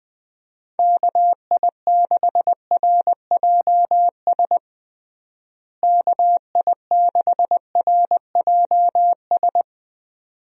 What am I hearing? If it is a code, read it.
Morse code, heard as a keyed tone, about 20 wpm, 700 Hz. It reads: KI6RJS KI6RJS
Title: USA Call Signs - 6 Characters - Repeated 2x - 20wpm